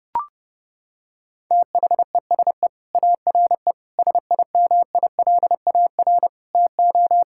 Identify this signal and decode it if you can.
Morse code, heard as a keyed tone, about 30 wpm, 700 Hz. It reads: THESE ARE SIMILAR TO
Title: Sentences from Top 1000 Words - Review Part 5 - 30wpm